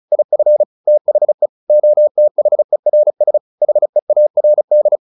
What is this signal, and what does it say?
Morse code, heard as a keyed tone, about 34 wpm, 600 Hz. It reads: IF THE OTHERS HEARD
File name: the old man and the sea / the old man and the sea 1 35wpm